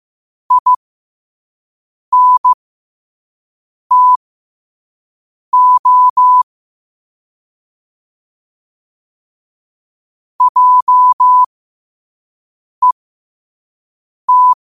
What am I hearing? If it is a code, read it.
Morse code, heard as a keyed tone, about 15 wpm, spaced as for 5 wpm, 1000 Hz. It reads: INTO JET
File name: qotdfiles / 5wpm